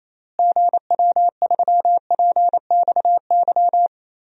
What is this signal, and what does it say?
Morse code, heard as a keyed tone, about 28 wpm, 700 Hz. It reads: GW3PXY
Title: DXCC-100 Call Signs - 6 Characters - Repeated 3x - 28wpm